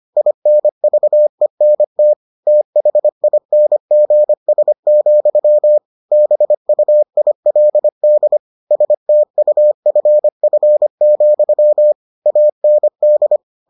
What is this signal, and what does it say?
Morse code, heard as a keyed tone, about 25 wpm, 600 Hz. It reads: INVENT THINGS, BUILD STUFF, AND